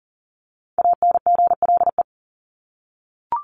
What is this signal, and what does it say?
Morse code, heard as a keyed tone, about 40 wpm, 700 Hz. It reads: ANGLE
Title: Top 1251-1500 Words 40wpm